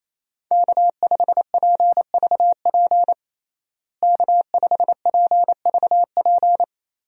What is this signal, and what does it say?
Morse code, heard as a keyed tone, about 28 wpm, 700 Hz. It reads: K5PVP K5PVP
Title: USA Call Signs - 5 Characters - Repeated 3x - 28wpm